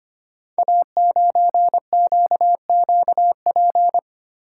Morse code, heard as a keyed tone, about 25 wpm, 700 Hz. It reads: A9QQP
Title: Code Groups - 5 Letter:Numbers - 25wpm